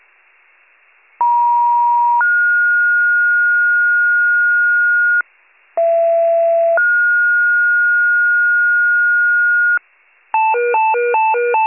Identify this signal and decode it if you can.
Two-tone page, heard as a one-second tone, then a three-second tone, about 950 then 1430 Hz, about 670 then 1430 Hz. It is heard from 1.2 to 9.8 s.